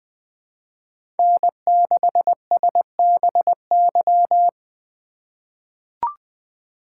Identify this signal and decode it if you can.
Morse code, heard as a keyed tone, about 20 wpm, 700 Hz. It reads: N6SBY
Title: USA Call Signs - 5 Characters - 4x Character Spacing - 20wpm